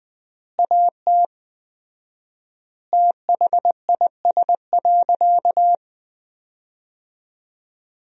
Morse code, heard as a keyed tone, about 21 wpm, 700 Hz. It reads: AT THIS.
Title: 4x Word-Spacing - Sentences from Top 300 Words - 20wpm